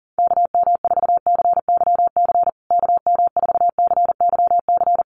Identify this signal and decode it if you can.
Morse code, heard as a keyed tone, about 37 wpm, 700 Hz. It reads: KM4CYC KM4CYC
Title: USA Call Signs Repeated 3x 40wpm